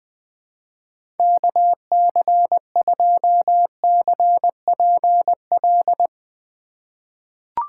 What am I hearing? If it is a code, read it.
Morse code, heard as a keyed tone, about 20 wpm, 700 Hz. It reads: KC2CPL
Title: USA Call Signs - 6 Characters - 4x Character Spacing - 20wpm